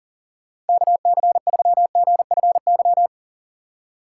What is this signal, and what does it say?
Morse code, heard as a keyed tone, about 49 wpm, 700 Hz. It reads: KC3GFY
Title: Speed Racing - USA 2x3 Call Signs 40wpm